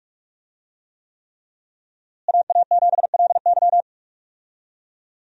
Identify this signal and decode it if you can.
Morse code, heard as a keyed tone, about 47 wpm, 700 Hz. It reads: AA7LY
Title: USA Call Signs - 5 Characters - Repeated 3x - 45wpm